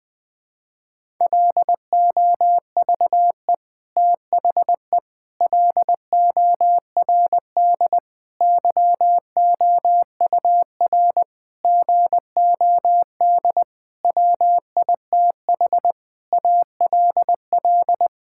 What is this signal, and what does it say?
Morse code, heard as a keyed tone, about 20 wpm, 700 Hz. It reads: LOVE THE LORD YOUR GOD WITH ALL
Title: Sentences with 9-Word Phrases - 20wpm